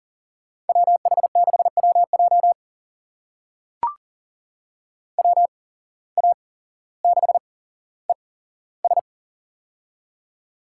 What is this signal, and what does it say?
Morse code, heard as a keyed tone, about 40 wpm, spaced as for 17 wpm, 700 Hz. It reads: WH6WJ WA6ES
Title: USA Call Signs - 5 Characters - 8x Character Spacing - 40wpm